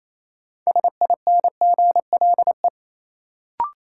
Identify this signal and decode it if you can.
Morse code, heard as a keyed tone, about 28 wpm, 700 Hz. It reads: SINGLE
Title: Sentences from Top 700 Words - New Word Drill - 28wpm